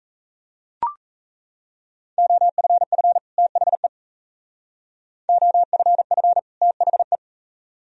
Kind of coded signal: Morse code

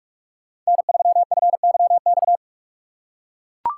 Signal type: Morse code

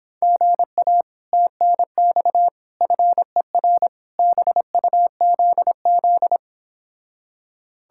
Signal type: Morse code

